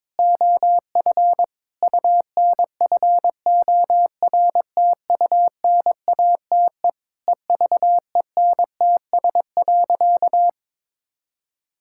Morse code, heard as a keyed tone, about 22 wpm, 700 Hz. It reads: OF UNFORTUNATE EVENTS.
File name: practice / 22wpm / Sentences with 8-Word Phrases - 22wpm